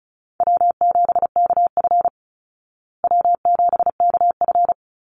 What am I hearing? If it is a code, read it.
Morse code, heard as a keyed tone, about 34 wpm, 700 Hz. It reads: W7KF W7KF
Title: Call Signs Repeated 3x - 35wpm